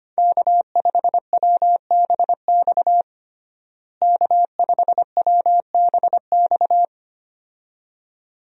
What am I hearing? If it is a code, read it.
Morse code, heard as a keyed tone, about 25 wpm, 700 Hz. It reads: K5WBX K5WBX
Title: USA Call Signs - 5 Characters - Repeated 2x - 25wpm